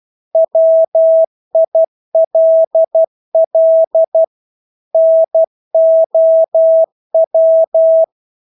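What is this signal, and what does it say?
Morse code, heard as a keyed tone, about 12 wpm, 650 Hz. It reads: WILL NOW
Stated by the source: Morse practice of English words at 12 wpm.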